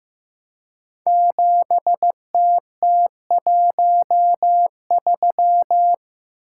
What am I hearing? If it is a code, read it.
Morse code, heard as a keyed tone, about 15 wpm, 700 Hz. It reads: 7TT13
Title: Code Groups - 5 Letter:Numbers - 15wpm